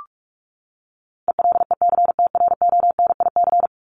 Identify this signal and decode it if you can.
Morse code, heard as a keyed tone, about 44 wpm, 700 Hz. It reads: ELECTRONIC
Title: Top 1001-1250 Words 45wpm